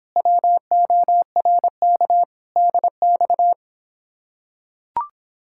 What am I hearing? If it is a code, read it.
Morse code, heard as a keyed tone, about 26 wpm, 700 Hz. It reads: WORK DX